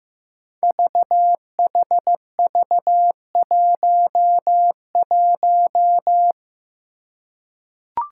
Morse code, heard as a keyed tone, about 15 wpm, 700 Hz. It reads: VHV11
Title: Five Challenging Letter-Number Combinations 15wpm